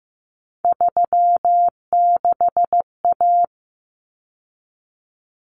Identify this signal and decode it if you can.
Morse code, heard as a keyed tone, about 15 wpm, 700 Hz. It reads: 36A
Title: Common Cut Numbers - 3-Digit 15wpm